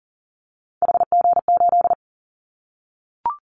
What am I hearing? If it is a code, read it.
Morse code, heard as a keyed tone, about 40 wpm, 700 Hz. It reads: HG8
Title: Callsign Prefixes for Top 50 DXCC - 40wpm